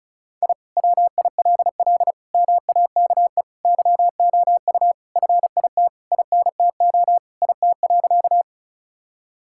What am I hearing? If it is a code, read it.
Morse code, heard as a keyed tone, about 36 wpm, 700 Hz. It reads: I WILL MAKE YOU FIT INTO IT.
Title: Sentences from Top 1000 Words - Review Part 5 - 35wpm